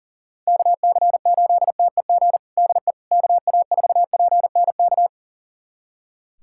Morse code, heard as a keyed tone, about 37 wpm, 700 Hz. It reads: KC8TEG DE KA4PNK
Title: USA Callsign DE Callsign 40wpm